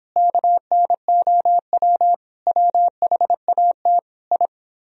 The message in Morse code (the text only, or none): KNOW WHAT I